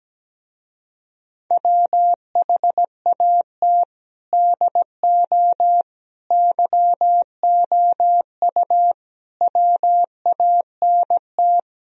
Morse code, heard as a keyed tone, about 17 wpm, 700 Hz. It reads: WHAT DO YOU WANT
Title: Speed-Racing - Sets of 5 Words 17wpm